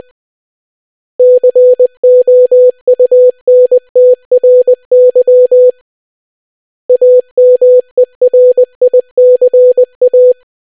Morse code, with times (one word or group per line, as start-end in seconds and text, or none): 1.2-5.7 COUNTRY
6.9-10.3 AMERICA